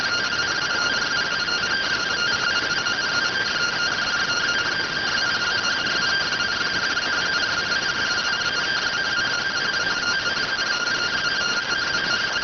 Two-tone teleprinter signal, about 170 Hz shift.